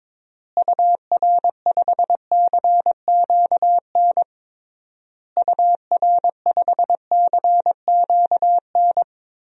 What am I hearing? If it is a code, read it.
Morse code, heard as a keyed tone, about 22 wpm, 700 Hz. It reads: UR5CQN UR5CQN
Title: DXCC-100 Call Signs - 6 Characters - Repeated 3x - 22wpm